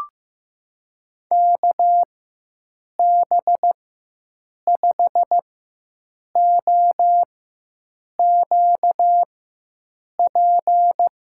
Morse code, heard as a keyed tone, about 15 wpm, spaced as for 7 wpm, 700 Hz. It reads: KB5OQP
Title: USA Call Signs - 6 Characters - 4x Character Spacing - 15wpm